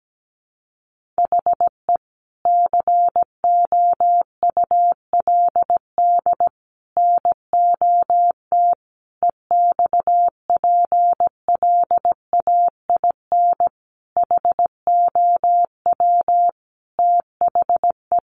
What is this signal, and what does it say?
Morse code, heard as a keyed tone, about 17 wpm, 700 Hz. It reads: HE COULD NOT EXPLAIN HOW THE
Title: Sentences from Top 1500 Words 17wpm